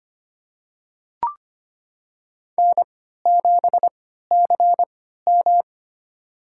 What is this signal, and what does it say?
Morse code, heard as a keyed tone, about 25 wpm, spaced as for 14 wpm, 700 Hz. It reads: N7CM
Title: USA Call Signs - 4 Characters - 3x Character Spacing - 25wpm